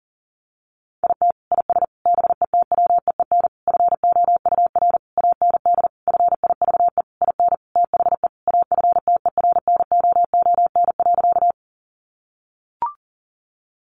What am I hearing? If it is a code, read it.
Morse code, heard as a keyed tone, about 40 wpm, 700 Hz. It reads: IT IS BETWEEN FOUR AND FIVE IN THE AFTERNOON.
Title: Sentences from Top 1500 Words 40wpm